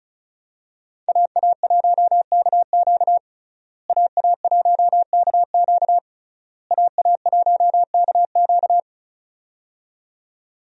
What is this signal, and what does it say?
Morse code, heard as a keyed tone, about 36 wpm, 700 Hz. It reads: AA1KQ AA1KQ AA1KQ
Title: USA Call Signs - 5 Characters - Repeated 3x - 35wpm